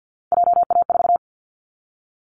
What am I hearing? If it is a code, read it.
Morse code, heard as a keyed tone, about 50 wpm, 700 Hz. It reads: JA4
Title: Callsign Prefixes for Top 25 DXCC - 50wpm